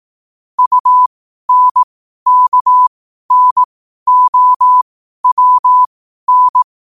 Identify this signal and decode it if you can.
Morse code, heard as a keyed tone, about 18 wpm, spaced as for 12 wpm, 1000 Hz. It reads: UNKNOWN
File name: qotdfiles / 12wpm